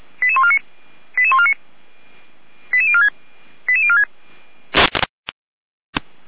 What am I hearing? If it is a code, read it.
Five-tone selective call (ZVEI), heard as quick runs of five tones, tones of about 70 ms. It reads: 88149 88149 88046 88046